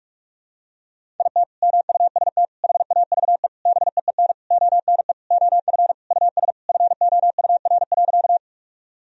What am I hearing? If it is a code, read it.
Morse code, heard as a keyed tone, about 41 wpm, 700 Hz. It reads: IT MUST HAVE BEEN ONE OF US FOUR.
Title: Sentences from Top 1000 Words - Review Part 1 - 45wpm